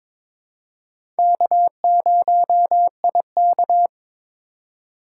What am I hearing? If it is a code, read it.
Morse code, heard as a keyed tone, about 22 wpm, 700 Hz. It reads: K0IK
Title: USA Call Signs - 4 Characters - 4x Character Spacing - 22wpm